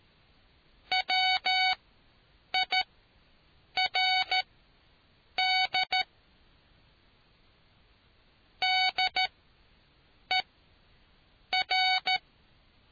Morse code, heard as a keyed tone, about 13 wpm, spaced as for 7 wpm, 750 Hz. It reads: WIRD DER